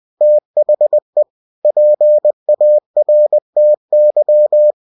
Morse code, heard as a keyed tone, about 20 wpm, 600 Hz. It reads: THE PARTY
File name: cw-books / winnie-the-pooh / winnie-20wpm-page-0103